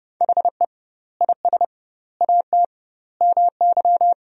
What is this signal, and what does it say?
Morse code, heard as a keyed tone, about 30 wpm, 700 Hz. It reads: HE IS AT MY
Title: 2x Word-Spacing - Sentences from Top 200 Words - 30wpm